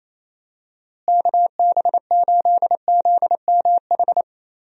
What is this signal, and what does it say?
Morse code, heard as a keyed tone, about 28 wpm, 700 Hz. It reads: KB8ZMH
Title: USA Call Signs - 6 Characters - Repeated 3x - 28wpm